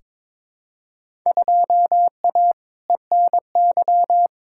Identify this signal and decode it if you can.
Morse code, heard as a keyed tone, about 22 wpm, 700 Hz. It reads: 2A ENY
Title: Field Day Exchanges - 22wpm